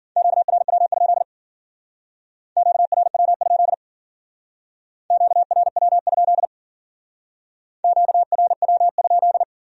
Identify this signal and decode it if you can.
Morse code, heard as a keyed tone, about 50 wpm, 700 Hz. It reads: QRW? QRW? QRW? QRW?
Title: Speed-Racing Q-Codes - 40wpm